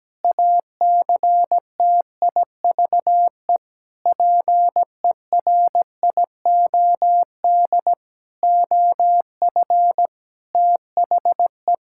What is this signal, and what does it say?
Morse code, heard as a keyed tone, about 17 wpm, 700 Hz. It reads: ACTIVE PERIOD OF THE
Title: Sentences from Top 1000 Words - Review Part 6 - 17wpm